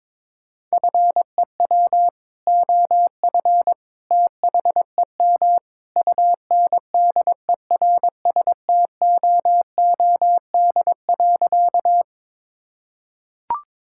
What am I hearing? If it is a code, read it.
Morse code, heard as a keyed tone, about 22 wpm, 700 Hz. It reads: FEW OF THEM UNDERSTOOD.
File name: practice / 22wpm / Sentences with 2-Word Phrases - 22wpm